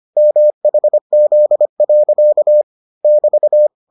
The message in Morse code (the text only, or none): MHZ. <BT>